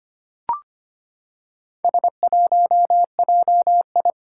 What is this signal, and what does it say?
Morse code, heard as a keyed tone, about 25 wpm, 700 Hz. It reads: S1JI
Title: Four Challenging Letter-Number Combinations 25wpm